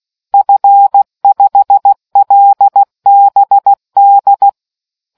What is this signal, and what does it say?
Morse code, heard as a keyed tone, about 16 wpm, 800 Hz. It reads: F5LBD